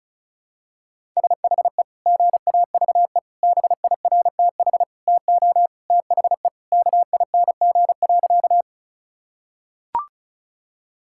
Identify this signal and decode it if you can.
Morse code, heard as a keyed tone, about 36 wpm, 700 Hz. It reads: SHE GAVE BIRTH TO THE KING.
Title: Sentences from Top 1000 Words - Review Part 5 - 35wpm